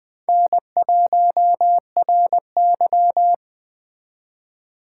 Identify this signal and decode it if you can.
Morse code, heard as a keyed tone, about 19 wpm, 700 Hz. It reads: N1RY